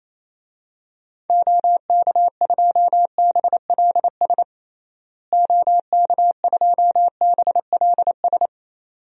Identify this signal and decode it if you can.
Morse code, heard as a keyed tone, about 28 wpm, 700 Hz. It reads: OK2BLS OK2BLS
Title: DXCC-100 Call Signs - 6 Characters - Repeated 2x - 28wpm